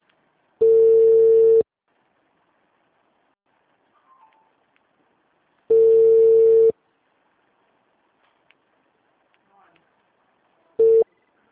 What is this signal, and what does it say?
Ringback tone (China, 450 Hz).